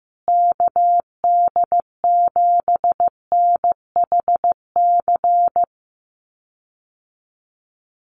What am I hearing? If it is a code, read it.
Morse code, heard as a keyed tone, about 15 wpm, 700 Hz. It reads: KD7NHC
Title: USA Call Signs Repeated 3x 15wpm